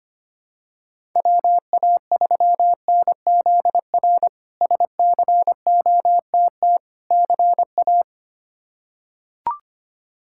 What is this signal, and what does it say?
Morse code, heard as a keyed tone, about 25 wpm, 700 Hz. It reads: WA3NZR SCOTT CA
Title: K1USN SST QSO Practice - 25wpm